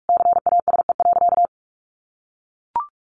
Morse code, heard as a keyed tone, about 44 wpm, 700 Hz. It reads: CASE.
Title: Sentences from Top 1400 Words 45wpm